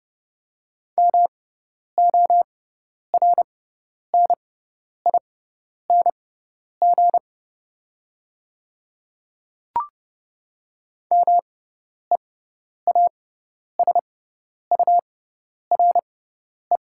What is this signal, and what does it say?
Morse code, heard as a keyed tone, about 29 wpm, spaced as for 10 wpm, 690 Hz. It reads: MORNING MEASURE
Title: 6x Character Spacing - Top 500 Words - 30wpm